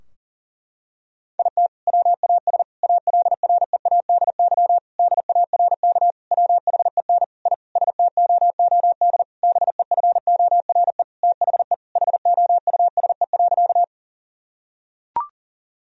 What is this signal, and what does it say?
Morse code, heard as a keyed tone, about 40 wpm, 700 Hz. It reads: IT WAS ALREADY DARK WHEN I STOOD BEFORE THE HOUSE.
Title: Sentences from Top 1000 Words - Review Part 2 - 40wpm